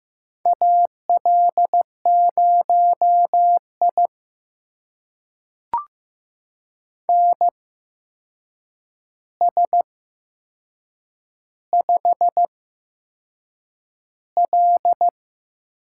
Morse code, heard as a keyed tone, about 15 wpm, spaced as for 6 wpm, 700 Hz. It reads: AL0I NS5L
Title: USA Call Signs - 4 Characters - 8x Character Spacing - 15wpm